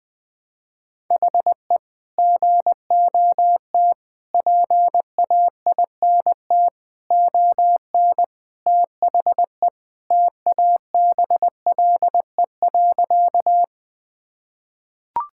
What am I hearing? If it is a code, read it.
Morse code, heard as a keyed tone, about 20 wpm, 700 Hz. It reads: HE GOT PAINT ON THE TABLE.
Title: Sentences from Top 1000 Words - Review Part 6 - 20wpm